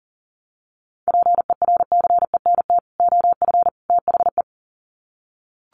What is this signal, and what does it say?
Morse code, heard as a keyed tone, about 40 wpm, 700 Hz. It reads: PERCENT OF THE
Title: Sets of 3 Words 40wpm